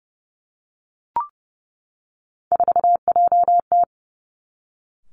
Morse code, heard as a keyed tone, about 30 wpm, 700 Hz. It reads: 4JT
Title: Three Challenging Letter-Number Combinations 30wpm